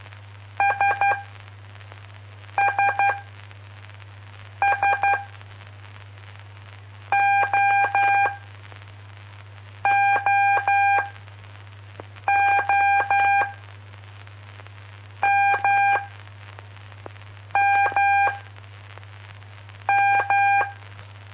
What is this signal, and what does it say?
Morse code, heard as a keyed tone, about 12 wpm, spaced as for 5 wpm, 800 Hz. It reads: SSS OOO MMM
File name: pp5bar / aulas de telegrafia / Aula-04